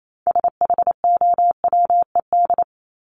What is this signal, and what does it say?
Morse code, heard as a keyed tone, about 28 wpm, 700 Hz. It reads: SHOWED
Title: ED Suffix - Top 100 Words with Base Words 28wpm